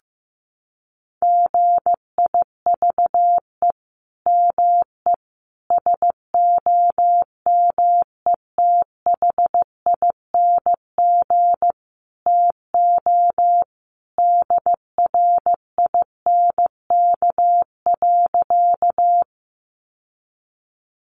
Morse code, heard as a keyed tone, about 14 wpm, 700 Hz. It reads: GIVE ME SOMETHING TO DRINK.
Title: Sentences from Top 1500 Words 15wpm